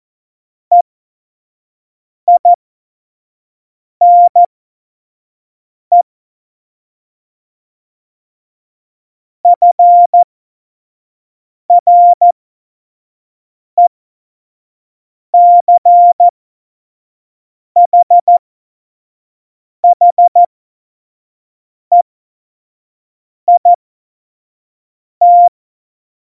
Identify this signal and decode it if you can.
Morse code, heard as a keyed tone, about 14 wpm, spaced as for 5 wpm, 700 Hz. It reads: EINE FRECHHEIT
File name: Morsekurs / Witze / Witz 15. Mai 2018, 25 F70